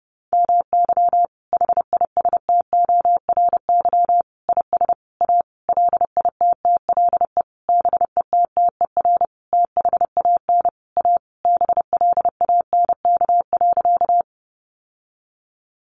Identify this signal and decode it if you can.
Morse code, heard as a keyed tone, about 30 wpm, 700 Hz. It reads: MY HISTORY IS A LITTLE BETTER THAN A BLANK.
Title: Sentences from Top 1500 Words 30wpm